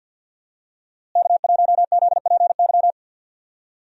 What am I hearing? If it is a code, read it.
Morse code, heard as a keyed tone, about 51 wpm, 700 Hz. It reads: K1ZPY
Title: USA Call Signs - 5 Characters - Repeated 3x - 50wpm